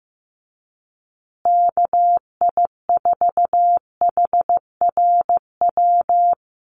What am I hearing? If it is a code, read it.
Morse code, heard as a keyed tone, about 15 wpm, 700 Hz. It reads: KI4HRW
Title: USA Call Signs Repeated 3x 15wpm